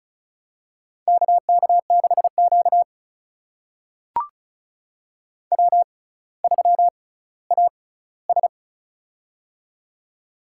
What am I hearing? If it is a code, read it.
Morse code, heard as a keyed tone, about 36 wpm, spaced as for 18 wpm, 700 Hz. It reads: KK6Q W3AS
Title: USA Call Signs - 4 Characters - 6x Character Spacing - 35wpm